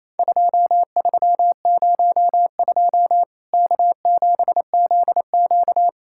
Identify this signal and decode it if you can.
Morse code, heard as a keyed tone, about 28 wpm, 700 Hz. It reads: 2302 K7ZQ
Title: Worked All Europe DX Contest - QTC Exchange - 28wpm